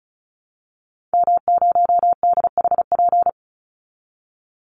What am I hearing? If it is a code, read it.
Morse code, heard as a keyed tone, about 34 wpm, 700 Hz. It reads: M0DHP
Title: Call Signs - 35wpm